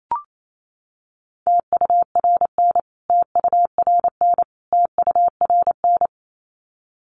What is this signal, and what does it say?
Morse code, heard as a keyed tone, about 28 wpm, 700 Hz. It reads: TURN TURN TURN
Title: Top 200 Words Repeated 28wpm